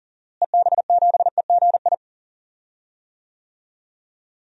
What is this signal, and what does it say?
Morse code, heard as a keyed tone, about 40 wpm, 700 Hz. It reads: EB7EGI
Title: DXCC-100 Call Signs - 6 Characters - Repeated 3x - 40wpm